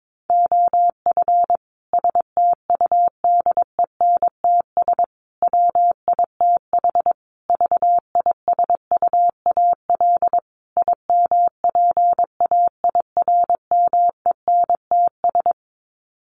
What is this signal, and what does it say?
Morse code, heard as a keyed tone, about 22 wpm, 700 Hz. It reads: OF STUDENTS WITH VISUAL IMPAIRMENTS
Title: Sets of 5 Words Encore 22wpm